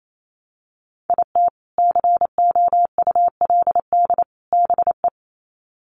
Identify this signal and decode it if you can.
Morse code, heard as a keyed tone, about 28 wpm, 700 Hz. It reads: IT COULD BE
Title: Sets of 3 Words 28wpm